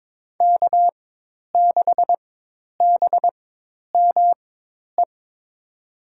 Morse code, heard as a keyed tone, about 22 wpm, spaced as for 10 wpm, 700 Hz. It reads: K6BME
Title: USA Call Signs - 5 Characters - 4x Character Spacing - 22wpm